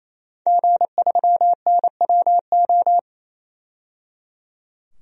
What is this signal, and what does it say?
Morse code, heard as a keyed tone, about 28 wpm, 700 Hz. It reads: G3NWO
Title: DXCC-100 Call Signs - 5 Characters - 28wpm